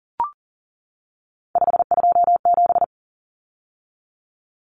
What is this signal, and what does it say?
Morse code, heard as a keyed tone, about 40 wpm, 700 Hz. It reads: 527
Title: Numbers - 3-Digit 40wpm